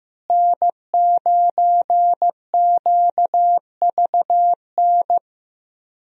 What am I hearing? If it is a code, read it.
Morse code, heard as a keyed tone, about 15 wpm, 700 Hz. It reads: N9QVN
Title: USA Call Signs - 5 Characters - Repeated 2x - 15wpm